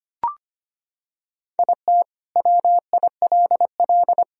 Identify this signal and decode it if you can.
Morse code, heard as a keyed tone, about 25 wpm, 700 Hz. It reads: IT WILL
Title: Sentences from Top 1000 Words - Review Part 1 - 25wpm